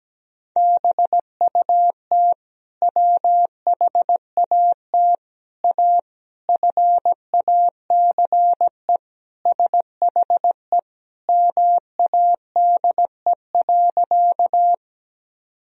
Morse code, heard as a keyed tone, about 17 wpm, 700 Hz. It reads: BUT WHAT A FACE SHE MADE.
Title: Sentences from Top 1000 Words - Review Part 2 - 17wpm